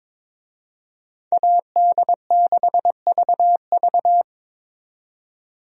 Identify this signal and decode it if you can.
Morse code, heard as a keyed tone, about 22 wpm, 700 Hz. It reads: AD6VV
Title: USA Call Signs - 5 Characters - 22wpm